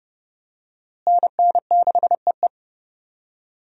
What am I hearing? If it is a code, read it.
Morse code, heard as a keyed tone, about 31 wpm, 700 Hz. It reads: NN6EE
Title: USA Call Signs - 5 Characters - Repeated 2x - 30wpm